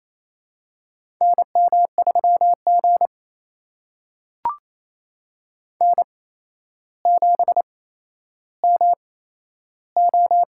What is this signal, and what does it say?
Morse code, heard as a keyed tone, about 28 wpm, spaced as for 12 wpm, 700 Hz. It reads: NM3G N7MO
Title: USA Call Signs - 4 Characters - 8x Character Spacing - 28wpm